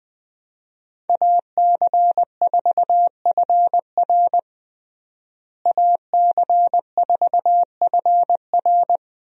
Morse code, heard as a keyed tone, about 20 wpm, 700 Hz. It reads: AC4FR AC4FR
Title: USA Call Signs - 5 Characters - Repeated 2x - 20wpm